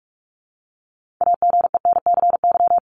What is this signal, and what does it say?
Morse code, heard as a keyed tone, about 44 wpm, 700 Hz. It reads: AGENCY
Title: Top 1001-1250 Words 45wpm